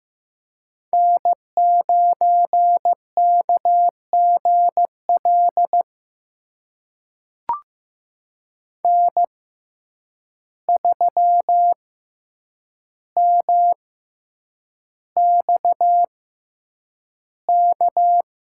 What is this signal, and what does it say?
Morse code, heard as a keyed tone, about 15 wpm, spaced as for 8 wpm, 700 Hz. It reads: N9KGL N3MXK
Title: USA Call Signs - 5 Characters - 6x Character Spacing - 15wpm